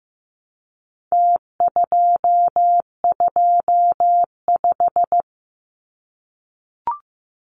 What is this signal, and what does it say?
Morse code, heard as a keyed tone, about 15 wpm, 700 Hz. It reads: T225
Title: Numbers - Contest Style - 4-Digit 15wpm